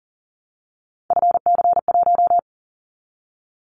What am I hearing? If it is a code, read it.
Morse code, heard as a keyed tone, about 40 wpm, 700 Hz. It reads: FC1
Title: Callsign Prefixes for Top 50 DXCC - 40wpm